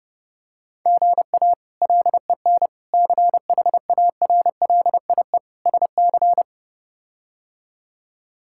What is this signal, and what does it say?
Morse code, heard as a keyed tone, about 30 wpm, 700 Hz. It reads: GA LEN CHARLIE SC
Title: K1USN SST QSO Practice - 30wpm